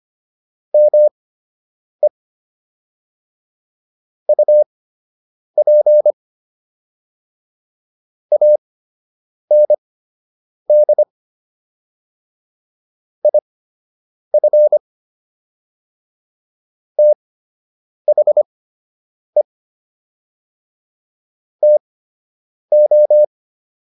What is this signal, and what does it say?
Morse code, heard as a keyed tone, about 25 wpm, spaced as for 8 wpm, 600 Hz. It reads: ME UP AND IF THE TO